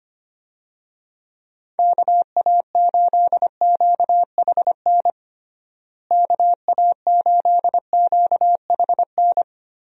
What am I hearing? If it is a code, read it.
Morse code, heard as a keyed tone, about 25 wpm, 700 Hz. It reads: KA8QHN KA8QHN
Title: USA Call Signs - 6 Characters - Repeated 3x - 25wpm